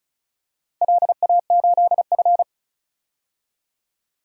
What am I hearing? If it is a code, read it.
Morse code, heard as a keyed tone, about 36 wpm, 700 Hz. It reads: LA8F
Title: DXCC-100 Call Signs - 4 Characters - 35wpm